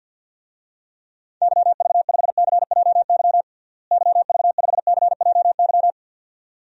Morse code, heard as a keyed tone, about 51 wpm, 700 Hz. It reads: YV5CJY YV5CJY
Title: DXCC-100 Call Signs - 6 Characters - Repeated 2x - 50wpm